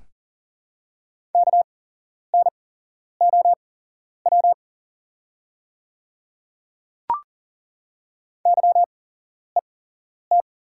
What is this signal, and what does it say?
Morse code, heard as a keyed tone, about 40 wpm, spaced as for 11 wpm, 700 Hz. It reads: KNOW YET
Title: 8x Character Spacing - Top 500 Words - 40wpm